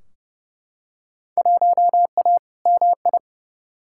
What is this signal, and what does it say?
Morse code, heard as a keyed tone, about 31 wpm, 700 Hz. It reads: 1A MI
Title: Field Day Exchanges - 30wpm